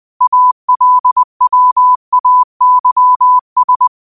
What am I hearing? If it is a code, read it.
Morse code, heard as a keyed tone, about 20 wpm, 1000 Hz. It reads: ALWAYS
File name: qotdfiles / 20wpm